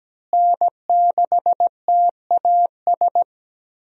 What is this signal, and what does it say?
Morse code, heard as a keyed tone, about 17 wpm, 700 Hz. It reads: N6TAS
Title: USA Call Signs - 5 Characters - Repeated 2x - 17wpm